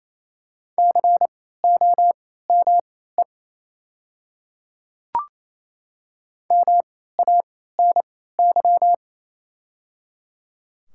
Morse code, heard as a keyed tone, about 28 wpm, spaced as for 16 wpm, 700 Hz. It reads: COME MANY
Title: 3x Character Spacing - Top 100 Words - 28wpm